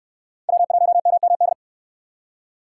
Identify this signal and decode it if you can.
Morse code, heard as a keyed tone, about 58 wpm, 700 Hz. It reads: W1MMD